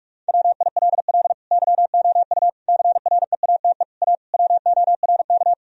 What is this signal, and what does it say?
Morse code, heard as a keyed tone, about 44 wpm, 700 Hz. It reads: WILL YOU CREATE A WORK
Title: Sentences from Top 1000 Words - Review Part 3 - 45wpm